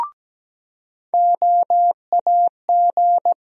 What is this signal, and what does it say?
Morse code, heard as a keyed tone, about 17 wpm, 700 Hz. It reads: OAG
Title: USA Call Sign Postfixes - 3 Letters 17wpm